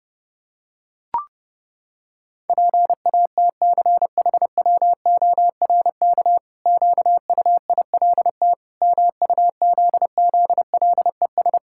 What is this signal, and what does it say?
Morse code, heard as a keyed tone, about 30 wpm, 700 Hz. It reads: PATCHWORK QUILT MUZZLES
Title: Pangram Challenge v2 - 30wpm